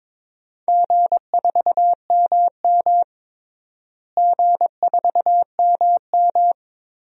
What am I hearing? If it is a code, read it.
Morse code, heard as a keyed tone, about 23 wpm, 700 Hz. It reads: G4MM G4MM
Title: DXCC-100 Call Signs - 4 Characters - Repeated 2x - 22wpm